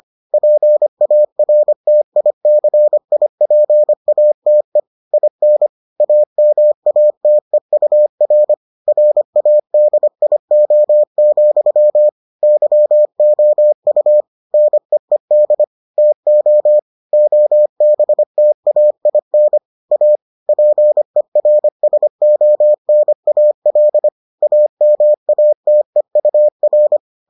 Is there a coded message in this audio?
Yes: Morse code from 0.3 to 27.0 s, heard as a keyed tone, about 25 wpm, 600 Hz. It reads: PARTICIPATE IN AMATEUR RADIO, YOU NEED TO OBTAIN A PERSONAL AMATEUR